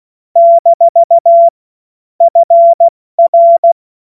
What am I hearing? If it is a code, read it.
Morse code, heard as a keyed tone, about 16 wpm, 680 Hz. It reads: - FR
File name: Morsekurs / Witze / Witz 06. September 2022 70 F80